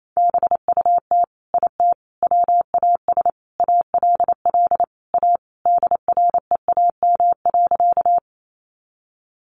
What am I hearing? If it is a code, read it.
Morse code, heard as a keyed tone, about 28 wpm, 700 Hz. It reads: BUT IT WAS ALL A DREAM.
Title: Sentences from Top 1500 Words 28wpm